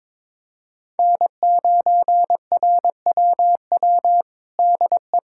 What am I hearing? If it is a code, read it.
Morse code, heard as a keyed tone, about 22 wpm, 700 Hz. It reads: N9RWW DE
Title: USA Callsign DE Callsign - 5 Characters - 22wpm